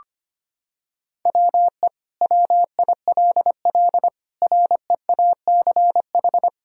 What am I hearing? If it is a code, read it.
Morse code, heard as a keyed tone, about 25 wpm, 700 Hz. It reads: WE WILL REACH
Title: Sentences from Top 1000 Words - Review Part 2 - 25wpm